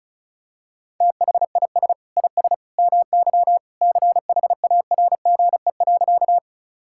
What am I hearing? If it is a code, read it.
Morse code, heard as a keyed tone, about 36 wpm, 700 Hz. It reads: THIS IS MY CHARGE.
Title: Sentences from Top 1000 Words - Review Part 6 - 35wpm